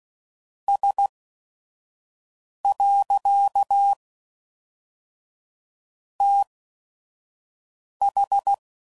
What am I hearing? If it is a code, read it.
Morse code, heard as a keyed tone, about 16 wpm, spaced as for 5 wpm, 800 Hz. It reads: S. TH
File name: CW Code Groups @5WPM / Technical Subjects / Reactance